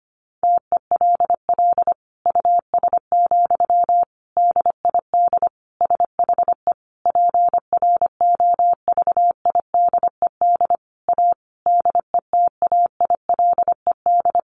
Morse code, heard as a keyed tone, about 25 wpm, 700 Hz. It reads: TELL US, DID SHE PROVIDED A DETAILED
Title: Sentences from Top 1500 Words 25wpm